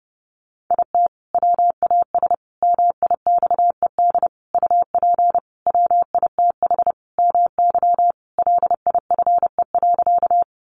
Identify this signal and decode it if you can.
Morse code, heard as a keyed tone, about 28 wpm, 700 Hz. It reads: IT WAS MIXED UP WITH MY LIFE.
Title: Sentences from Top 1500 Words 30wpm